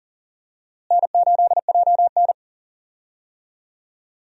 Morse code, heard as a keyed tone, about 40 wpm, 700 Hz. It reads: N8JN
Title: USA Call Signs - 4 Characters - Repeated 2x - 40wpm